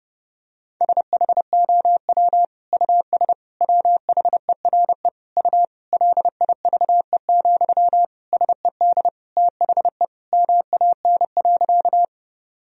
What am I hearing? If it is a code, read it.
Morse code, heard as a keyed tone, about 32 wpm, 700 Hz. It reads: SHOW US WHERE U LIVE, SED THE MAN.
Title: Sentences from Top 200 Words and Abbreviations 30wpm